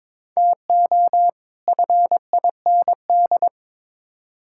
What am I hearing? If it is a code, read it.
Morse code, heard as a keyed tone, about 22 wpm, 700 Hz. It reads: TO FIND
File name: Build A Phrase - 2-Word Phrases - 22wpm